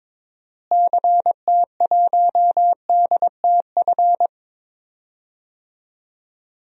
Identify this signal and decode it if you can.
Morse code, heard as a keyed tone, about 22 wpm, 700 Hz. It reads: CT1DTF
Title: DXCC-100 Call Signs - 6 Characters - Repeated 2x - 22wpm